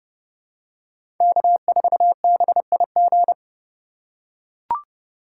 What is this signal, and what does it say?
Morse code, heard as a keyed tone, about 30 wpm, 700 Hz. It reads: K4BIG
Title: USA Call Signs - 5 Characters - 2x Character Spacing - 30wpm